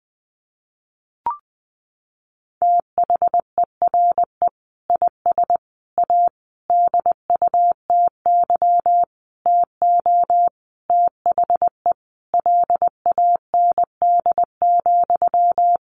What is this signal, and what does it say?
Morse code, heard as a keyed tone, about 20 wpm, 700 Hz. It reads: THERE IS A DUTY TO THE LAND,
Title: Sentences from Top 1500 Words 20wpm